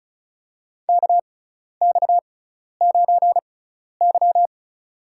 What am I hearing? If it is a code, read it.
Morse code, heard as a keyed tone, about 34 wpm, spaced as for 12 wpm, 700 Hz. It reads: KX9Y
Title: USA Call Signs - 4 Characters - 6x Character Spacing - 35wpm